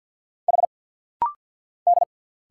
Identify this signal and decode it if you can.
Morse code, heard as a keyed tone, about 51 wpm, 700 Hz. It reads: H D
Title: Single Letter-Number Alt - Rapid-Fire 50wpm